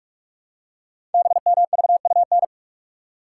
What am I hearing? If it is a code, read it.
Morse code, heard as a keyed tone, about 44 wpm, 700 Hz. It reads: BMVUN
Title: Code Groups - 5 Letters - 45wpm